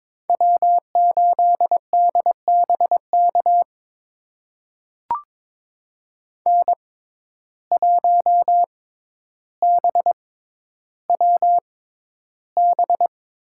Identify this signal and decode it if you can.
Morse code, heard as a keyed tone, about 22 wpm, spaced as for 11 wpm, 700 Hz. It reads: W8DBK N1BWB
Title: USA Call Signs - 5 Characters - 6x Character Spacing - 22wpm